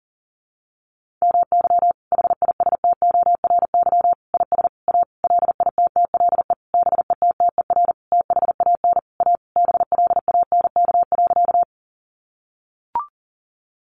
Morse code, heard as a keyed tone, about 40 wpm, 700 Hz. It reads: MY HISTORY IS A LITTLE BETTER THAN A BLANK.
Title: Sentences from Top 1500 Words 40wpm